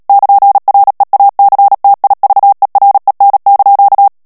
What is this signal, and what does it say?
Morse code, heard as a keyed tone, about 37 wpm, 800 Hz. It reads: <KN>REACTIVEREN)
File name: morse mp3 / HOLLANDAIS texte clair 2 vitesse 35 mots minute